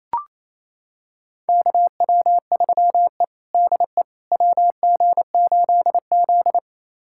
Morse code, heard as a keyed tone, about 28 wpm, 700 Hz. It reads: KW3E DE WG8Z
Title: USA Callsign DE Callsign - 4 Characters - 28wpm